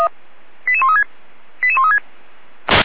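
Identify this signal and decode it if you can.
Five-tone selective call (ZVEI), heard as quick runs of five tones, tones of about 70 ms. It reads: 88136 88136
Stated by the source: Five-tone ZVEI selective call received by SDR, archived by Feuerwehr Orlamünde (fire brigade).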